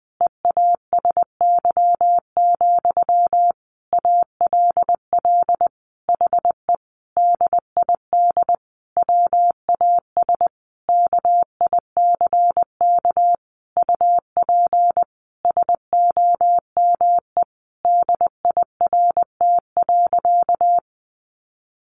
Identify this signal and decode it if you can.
Morse code, heard as a keyed tone, about 20 wpm, 700 Hz. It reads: EASY, ALL HE DID WAS KICK UP SOME DIRT.
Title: A Sound Of Thunder 20wpm